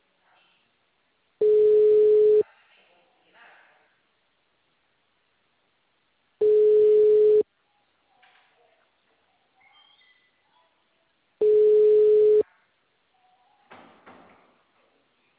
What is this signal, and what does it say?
Ringback tone (Europe, 425 Hz), with speech.
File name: out- -2014- - - .40766